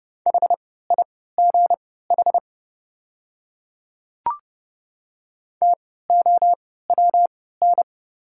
Morse code, heard as a keyed tone, about 30 wpm, spaced as for 17 wpm, 690 Hz. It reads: HIGH TOWN
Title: 3x Character Spacing - Top 500 Words - 30wpm